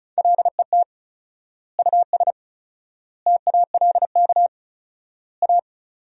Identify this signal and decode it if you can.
Morse code, heard as a keyed tone, about 34 wpm, 700 Hz. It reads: LET US TALK A
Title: 4x Word-Spacing - Sentences from Top 400 Words - 35wpm